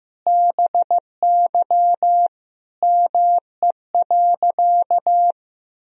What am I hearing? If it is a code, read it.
Morse code, heard as a keyed tone, about 14 wpm, 700 Hz. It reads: BY ME.
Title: Sentences from Top 500 Words and Abbreviations 15wpm